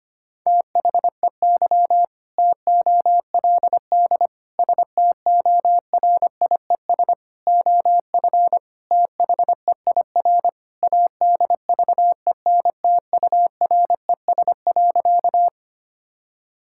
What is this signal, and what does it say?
Morse code, heard as a keyed tone, about 25 wpm, 700 Hz. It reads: THEY TOLD STORIES OF THEIR ADVENTURES.
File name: practice / 25wpm / Sentences with 2-Word Phrases - 25wpm